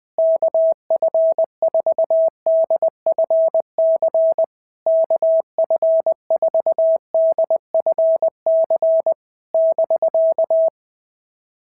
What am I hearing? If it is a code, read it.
Morse code, heard as a keyed tone, about 20 wpm, 650 Hz. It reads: KF4DFC KF4DFC <BK>
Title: SOTA QSO Practice 20wpm